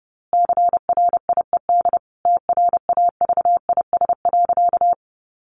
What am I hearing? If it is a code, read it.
Morse code, heard as a keyed tone, about 30 wpm, 700 Hz. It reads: CRIED TRAVIS.
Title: A Sound Of Thunder 30wpm